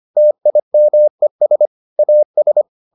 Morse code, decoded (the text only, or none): TIMES AS